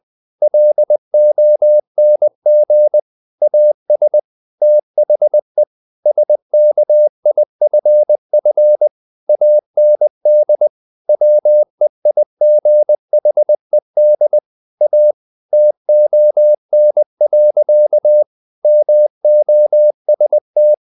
Morse code, heard as a keyed tone, about 20 wpm, 600 Hz. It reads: LONG AS THE SKIFF AND WEIGHED A TON. MOST
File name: the old man and the sea / the old man and the sea part 1 20wpm